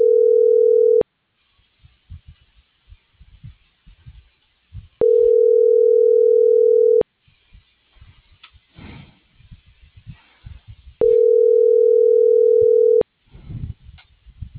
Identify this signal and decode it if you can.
Ringback tone (North America), 440 + 480 Hz.